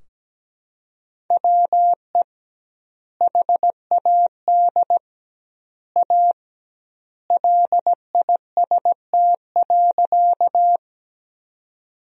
Morse code, heard as a keyed tone, about 17 wpm, 700 Hz. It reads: WE HAD A LIST.
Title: 2x Word-Spacing - Sentences from Top 400 Words - 17wpm